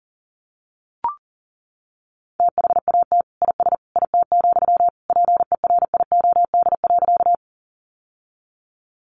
Morse code, heard as a keyed tone, about 40 wpm, 700 Hz. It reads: THAT IS IT, PERIOD.
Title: Sentences from Top 800 Words 40wpm